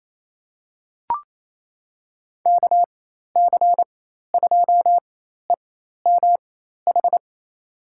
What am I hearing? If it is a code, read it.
Morse code, heard as a keyed tone, about 28 wpm, spaced as for 13 wpm, 700 Hz. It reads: KC2EMH
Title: USA Call Signs - 6 Characters - 4x Character Spacing - 28wpm